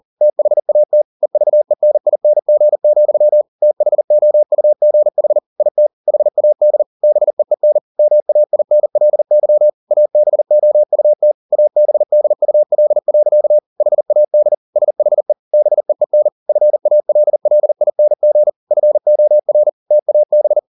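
Morse code, heard as a keyed tone, about 40 wpm, 600 Hz. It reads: THAT EVENING, THOUGH IT HAD BEEN MAINLY ABOUT ABDUL. HAD SHE BEEN FALLING FOR TAB
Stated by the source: Morse code at 40 wpm, practice text, CW med Gnister.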